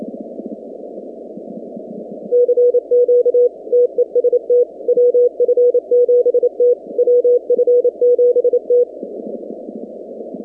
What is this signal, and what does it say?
Morse code, heard as a keyed tone, about 28 wpm, 490 Hz. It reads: CQ TEST WF7T WF7T
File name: 2019-cw-wpx-kt5j-k5pi